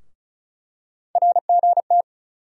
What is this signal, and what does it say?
Morse code, heard as a keyed tone, about 36 wpm, 700 Hz. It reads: RGT
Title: USA Call Sign Postfixes - 3 Letters 35wpm